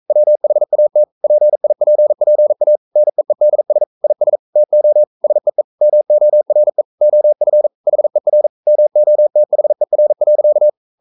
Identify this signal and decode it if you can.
Morse code, heard as a keyed tone, about 42 wpm, 600 Hz. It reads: WHAT PIPPA NEEDS IS TO SEE MORE OF HER MOTHER.
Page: Romanen Never af Ken Follet på engelsk.